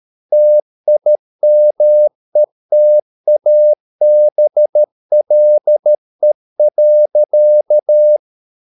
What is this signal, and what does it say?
Morse code, heard as a keyed tone, about 13 wpm, 600 Hz. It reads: TIMETABLE.